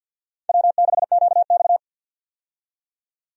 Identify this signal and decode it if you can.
Morse code, heard as a keyed tone, about 53 wpm, 700 Hz. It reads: W6QX
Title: USA Call Signs - 4 Characters - 50wpm